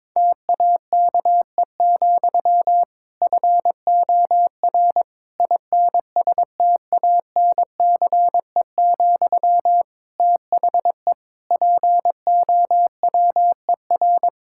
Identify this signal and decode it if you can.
Morse code, heard as a keyed tone, about 22 wpm, 700 Hz. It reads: TAKE, FOR INSTANCE, THE POWER
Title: Sentences from Top 1000 Words - Review Part 5 - 22wpm